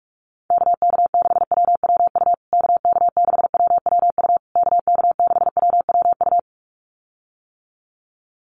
Morse code, heard as a keyed tone, about 44 wpm, 700 Hz. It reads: KK6WWU KK6WWU KK6WWU
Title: USA Call Signs Repeated 3x 45wpm